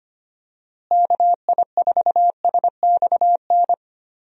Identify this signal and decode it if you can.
Morse code, heard as a keyed tone, about 25 wpm, 700 Hz. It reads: KI4SXN
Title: USA Call Signs - 6 Characters - Repeated 3x - 25wpm